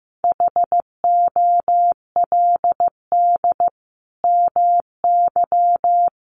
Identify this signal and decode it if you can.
Morse code, heard as a keyed tone, about 15 wpm, 700 Hz. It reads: HOLD MY
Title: Sentences from Top 1200 Words 15wpm